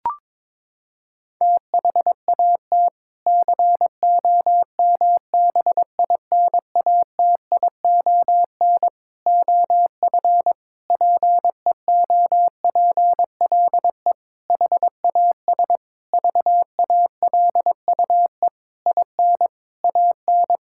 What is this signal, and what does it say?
Morse code, heard as a keyed tone, about 22 wpm, 700 Hz. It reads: THAT COMBINATION OF PEOPLE HAS VALUE IN AN
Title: Sentences from Top 1000 Words - Review Part 6 - 22wpm